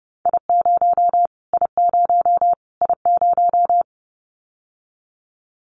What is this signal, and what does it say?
Morse code, heard as a keyed tone, about 30 wpm, 700 Hz. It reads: I0 I0 I0
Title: Callsign Prefixes for Top 25 DXCC - Repeated 3x - 30wpm